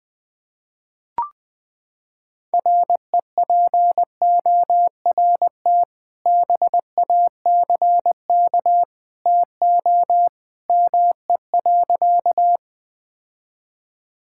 Morse code, heard as a keyed tone, about 20 wpm, 700 Hz. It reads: REPORT BACK TO ME.
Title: Sentences from Top 1000 Words - Review Part 6 - 20wpm